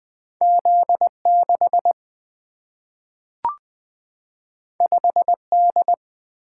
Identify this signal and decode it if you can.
Morse code, heard as a keyed tone, about 20 wpm, 700 Hz. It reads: Z 6 5 D
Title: Single-Letter Single-Number 20wpm